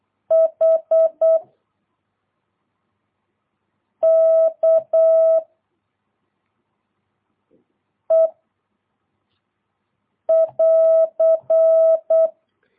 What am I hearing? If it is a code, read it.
Morse code, heard as a keyed tone, about 8 wpm, 660 Hz. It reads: H K E <AR>
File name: jh8cbh / 2022cw